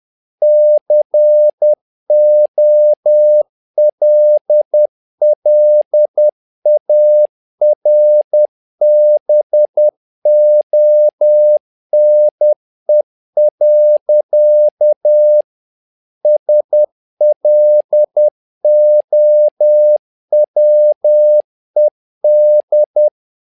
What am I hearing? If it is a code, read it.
Morse code, heard as a keyed tone, about 10 wpm, 600 Hz. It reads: COLLARBONE. SLOWED
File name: learning center / tower trouble / tower trouble 10wpm